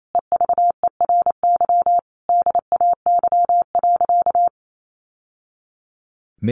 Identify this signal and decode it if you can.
Morse code, heard as a keyed tone, about 28 wpm, 700 Hz. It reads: EVERY DAY.
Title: Sentences from Top 1100 Words 28wpm